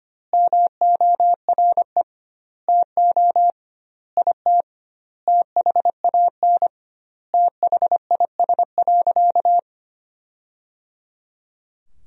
Morse code, heard as a keyed tone, about 25 wpm, 700 Hz. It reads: MORE TO IT THAN THIS.
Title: 2x Word-Spacing - Sentences from Top 100 Words - 25wpm